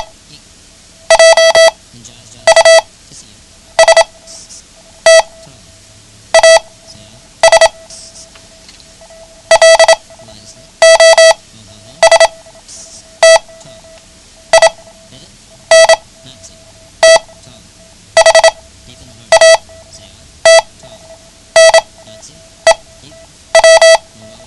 Morse code, heard as a keyed tone, about 27 wpm, spaced as for 8 wpm, 660 Hz. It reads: JUST AS LOST IN THAT NEW
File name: ham radio / CodeQuick / Code Quick 09